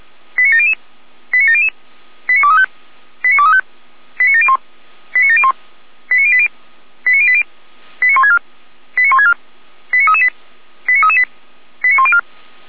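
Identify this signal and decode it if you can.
Five-tone selective call (ZVEI), heard as quick runs of five tones, tones of about 70 ms. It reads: 89700 89700 89235 89235 89701 89701 89080 89080 89164 89164 89338 89338 89104
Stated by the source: ZVEI five-tone selective call received by SDR, from feuerwehr-orlamuende.org fire brigade audio archive.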